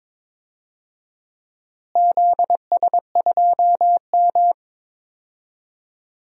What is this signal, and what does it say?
Morse code, heard as a keyed tone, about 22 wpm, 700 Hz. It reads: ZS2M
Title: DXCC-100 Call Signs - 4 Characters - Repeated 3x - 22wpm